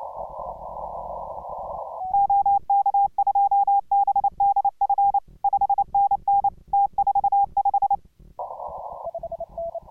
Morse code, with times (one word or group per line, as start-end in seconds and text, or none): 2.1-5.2 OK2BDF
5.4-6.5 5NN
6.7-8.0 T45